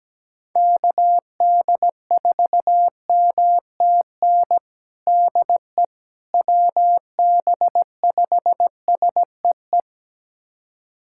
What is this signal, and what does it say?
Morse code, heard as a keyed tone, about 17 wpm, 700 Hz. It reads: KD4MTN DE WB5SEE
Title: USA Callsign DE Callsign - 6 Characters - 17wpm